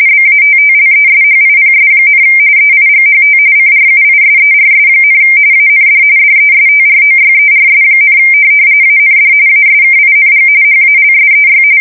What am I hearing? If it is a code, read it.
RTTY, about 45.45 baud. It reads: BATTLESHIPS, EIGHT CRUISERS, AND DESTROYERS, AND ASKING HALSEY
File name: fsk170 / USS-New-Jersey